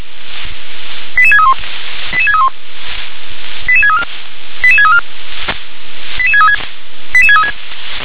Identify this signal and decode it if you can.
Five-tone selective call (ZVEI), heard as quick runs of five tones, tones of about 70 ms. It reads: 88631 88631 88634 88634 88636 88636